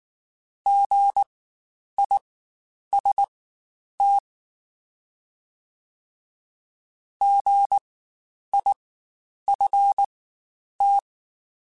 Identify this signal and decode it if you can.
Morse code, heard as a keyed tone, about 19 wpm, spaced as for 9 wpm, 800 Hz. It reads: GIST GIFT
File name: Spaced Alphas / Spaced Alphas mp3s / Gs sp @18